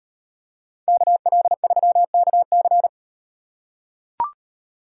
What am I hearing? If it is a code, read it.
Morse code, heard as a keyed tone, about 37 wpm, 700 Hz. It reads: KL3KC